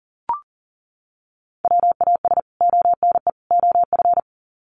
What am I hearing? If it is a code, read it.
Morse code, heard as a keyed tone, about 40 wpm, 700 Hz. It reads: WAS ONE OF
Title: Sets of 3 Words 40wpm